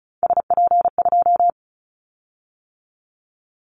Morse code, heard as a keyed tone, about 34 wpm, 700 Hz. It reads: SP2
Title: Callsign Prefixes for Top 25 DXCC - 35wpm